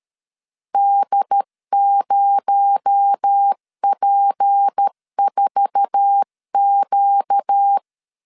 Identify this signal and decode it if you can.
Morse code, heard as a keyed tone, about 13 wpm, 800 Hz. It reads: D0P4Q
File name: LEZIONI / 9 28 12wpm 2351